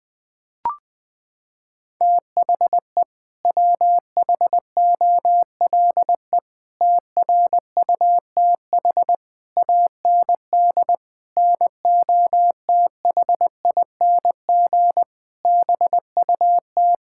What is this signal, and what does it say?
Morse code, heard as a keyed tone, about 20 wpm, 700 Hz. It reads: THE WHOLE TRUTH AND NOTHING BUT
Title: Sets of 8 Words - 20wpm